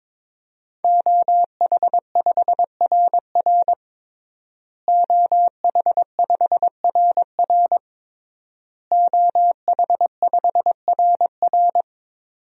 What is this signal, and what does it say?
Morse code, heard as a keyed tone, about 22 wpm, 700 Hz. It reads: OH5RR OH5RR OH5RR
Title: DXCC-100 Call Signs - 5 Characters - Repeated 3x - 22wpm